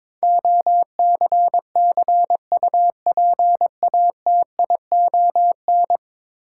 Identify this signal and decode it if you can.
Morse code, heard as a keyed tone, about 22 wpm, 700 Hz. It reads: OCCUPATION
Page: morse-code-ninja-practice-22wpm